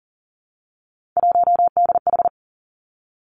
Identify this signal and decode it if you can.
Morse code, heard as a keyed tone, about 40 wpm, 700 Hz. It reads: 1DH
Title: Three Challenging Letter-Number Combinations 40wpm